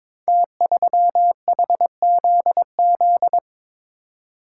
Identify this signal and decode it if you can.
Morse code, heard as a keyed tone, about 22 wpm, 700 Hz. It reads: T3HZZ
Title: Code Groups - 5 Letter:Numbers - 22wpm